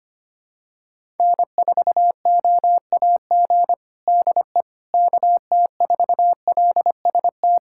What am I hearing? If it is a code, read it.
Morse code, heard as a keyed tone, about 25 wpm, 700 Hz. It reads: N4OAG DE KT4LST
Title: USA Callsign DE Callsign 25wpm